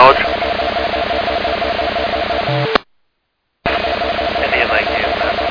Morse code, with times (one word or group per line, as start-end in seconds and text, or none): none